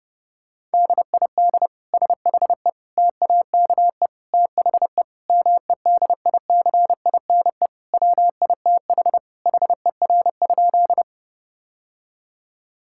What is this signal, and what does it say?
Morse code, heard as a keyed tone, about 30 wpm, 700 Hz. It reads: DID SHE TAKE THE MEDICINE WITH HER?
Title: Sentences from Top 1000 Words - Review Part 6 - 30wpm